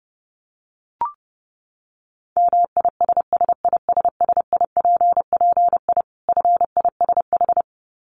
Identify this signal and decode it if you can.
Morse code, heard as a keyed tone, about 30 wpm, 700 Hz. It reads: MISSISSIPPI FISH
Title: Dit Challenge 30wpm